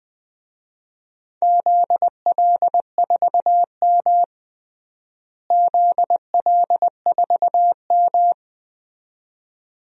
Morse code, heard as a keyed tone, about 20 wpm, 700 Hz. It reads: ZL4M ZL4M
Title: DXCC-100 Call Signs - 4 Characters - Repeated 2x - 20wpm